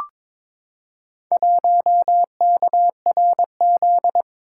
Morse code, heard as a keyed tone, about 21 wpm, 700 Hz. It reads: 1KRZ
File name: practice / 22wpm / Four Letter-Number Combinations - 22wpm